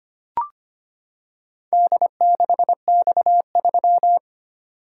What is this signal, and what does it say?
Morse code, heard as a keyed tone, about 25 wpm, 700 Hz. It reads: D6X3
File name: practice / 25wpm / Four Letter-Number Combinations - 25wpm